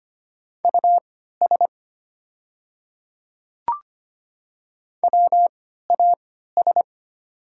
Morse code, heard as a keyed tone, about 25 wpm, spaced as for 14 wpm, 690 Hz. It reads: US WAS
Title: 3x Character Spacing - Top 300 Words - 25wpm